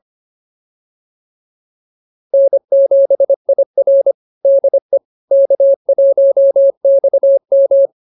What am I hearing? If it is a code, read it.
Morse code, heard as a keyed tone, about 24 wpm, 550 Hz. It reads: N7IR DE K1XM
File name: QSO103 25